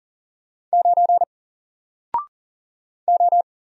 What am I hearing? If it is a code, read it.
Morse code, heard as a keyed tone, about 40 wpm, 700 Hz. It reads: 9 O
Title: Single Letter-Number 40wpm